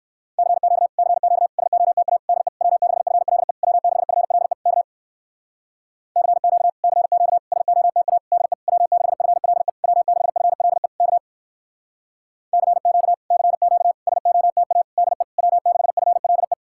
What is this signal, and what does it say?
Morse code, heard as a keyed tone, about 59 wpm, 700 Hz. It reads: CQ CQ SOTA DE W6FLE W6FLE K CQ CQ SOTA DE W6FLE W6FLE K CQ CQ SOTA DE W6FLE